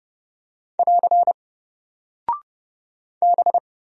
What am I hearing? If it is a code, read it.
Morse code, heard as a keyed tone, about 30 wpm, 700 Hz. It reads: <AR> B
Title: Single Character (Letter:Number:Punctuation:Prosign) Alt 30wpm